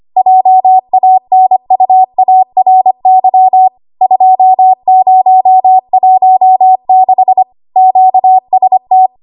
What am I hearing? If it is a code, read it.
Morse code, heard as a keyed tone, about 25 wpm, 750 Hz. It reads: JANUARY 2016 QST
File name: Archive / 25 WPM / 25WPM